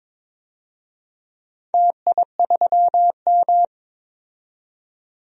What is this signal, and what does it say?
Morse code, heard as a keyed tone, about 22 wpm, 700 Hz. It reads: TI3M
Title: DXCC-100 Call Signs - 4 Characters - 22wpm